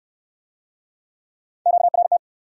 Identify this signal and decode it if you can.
Morse code, heard as a keyed tone, about 64 wpm, 700 Hz. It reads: QRT